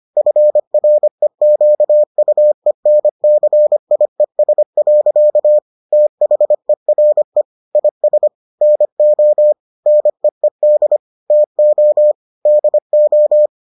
Morse code, heard as a keyed tone, about 25 wpm, 600 Hz. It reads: FREQUENCIES. THERE IS NO NEED TO DO